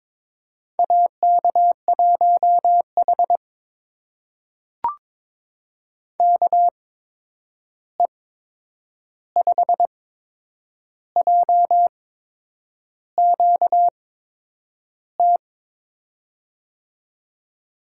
Morse code, heard as a keyed tone, about 22 wpm, spaced as for 8 wpm, 700 Hz. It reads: AK1H KE5JQT
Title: USA Call Signs - 8x Character Spacing - 22wpm